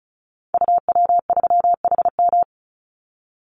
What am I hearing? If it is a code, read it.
Morse code, heard as a keyed tone, about 34 wpm, 700 Hz. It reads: UW3HM
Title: Call Signs Repeated 3x - 35wpm